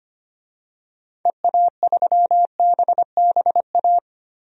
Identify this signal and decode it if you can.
Morse code, heard as a keyed tone, about 25 wpm, 700 Hz. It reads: EA3BBA
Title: DXCC-100 Call Signs - 6 Characters - 25wpm